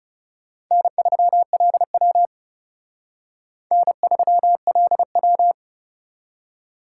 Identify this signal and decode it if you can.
Morse code, heard as a keyed tone, about 40 wpm, 700 Hz. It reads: N3LW N3LW